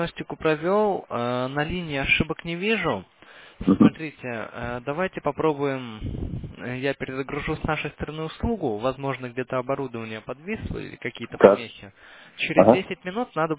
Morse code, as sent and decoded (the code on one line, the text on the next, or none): none